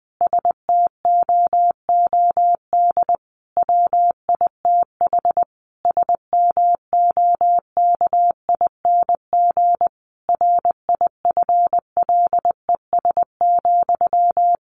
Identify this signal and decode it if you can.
Morse code, heard as a keyed tone, about 20 wpm, 700 Hz. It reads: STOOD WITH SMOKING RIFLES,
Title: A Sound Of Thunder 20wpm